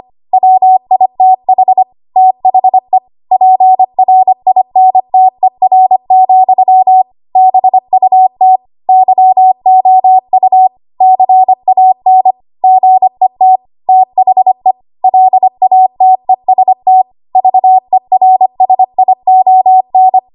Morse code, heard as a keyed tone, about 25 wpm, 750 Hz. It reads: WITH THE PRINTER, BUT YOU CAN GET THE LATEST VERSION
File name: Archive / 25 WPM / 25WPM